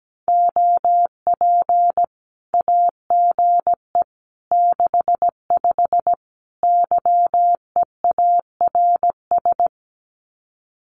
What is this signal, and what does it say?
Morse code, heard as a keyed tone, about 17 wpm, 700 Hz. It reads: OP AGE 65 YEARS
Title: Common QSO Phrases 17wpm